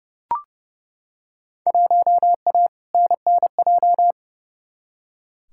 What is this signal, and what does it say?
Morse code, heard as a keyed tone, about 29 wpm, 700 Hz. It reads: 1A NNJ
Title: Field Day Exchanges Encore - 30wpm